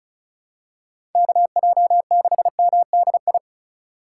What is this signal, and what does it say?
Morse code, heard as a keyed tone, about 36 wpm, 700 Hz. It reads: KJ6MDI
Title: USA Call Signs - 6 Characters - 2x Character Spacing - 35wpm